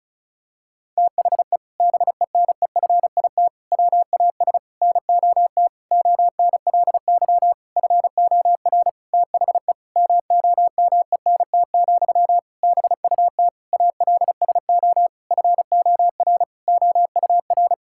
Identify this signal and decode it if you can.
Morse code, heard as a keyed tone, about 36 wpm, 700 Hz. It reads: THE BENEFIT WAS NOT ONLY FOR THE MOMENT, BUT ALSO FOR OUR
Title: Sentences from Top 1000 Words - Review Part 6 - 35wpm